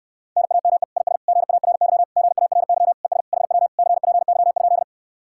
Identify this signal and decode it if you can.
Morse code, heard as a keyed tone, about 70 wpm, 700 Hz. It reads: NAME IS GARY GARY ES HW CPY?